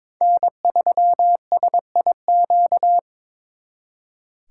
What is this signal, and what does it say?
Morse code, heard as a keyed tone, about 26 wpm, 700 Hz. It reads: N3SIQ
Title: Speed Racing - USA 1x3 Call Signs 22wpm